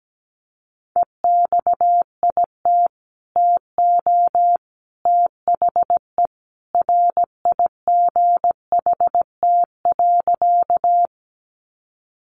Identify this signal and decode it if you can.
Morse code, heard as a keyed tone, about 17 wpm, 700 Hz. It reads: EXIT TO THE RIGHT.
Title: Sentences from Top 1100 Words 17wpm